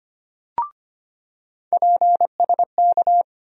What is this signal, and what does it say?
Morse code, heard as a keyed tone, about 25 wpm, 700 Hz. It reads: PSK
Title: USA Call Sign Postfixes - 3 Letters 25wpm